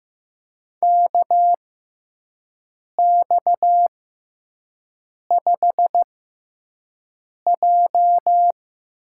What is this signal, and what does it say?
Morse code, heard as a keyed tone, about 15 wpm, spaced as for 5 wpm, 700 Hz. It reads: KX5J
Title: USA Call Signs - 6x Character Spacing - 15wpm